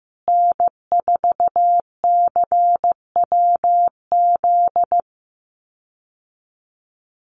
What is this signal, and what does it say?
Morse code, heard as a keyed tone, about 15 wpm, 700 Hz. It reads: N4CWZ
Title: USA Call Signs Repeated 3x 15wpm